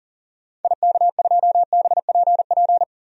Morse code, heard as a keyed tone, about 37 wpm, 700 Hz. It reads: IK2BPP
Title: DXCC-100 Call Signs - 6 Characters - 40wpm